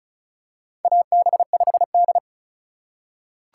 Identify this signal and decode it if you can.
Morse code, heard as a keyed tone, about 36 wpm, 700 Hz. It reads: AB5D
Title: USA Call Signs - 4 Characters - 4x Character Spacing - 35wpm